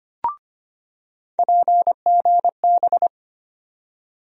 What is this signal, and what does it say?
Morse code, heard as a keyed tone, about 25 wpm, 700 Hz. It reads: PGB
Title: USA Call Sign Postfixes - 3 Letters 25wpm